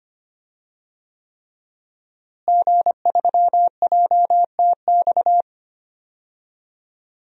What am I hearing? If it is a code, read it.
Morse code, heard as a keyed tone, about 25 wpm, 700 Hz. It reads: G3JTX
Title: DXCC-100 Call Signs - 5 Characters - Repeated 3x - 25wpm